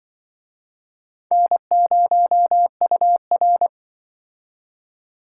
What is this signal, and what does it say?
Morse code, heard as a keyed tone, about 23 wpm, 700 Hz. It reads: N0UR
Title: Speed Racing - USA Call Signs 20wpm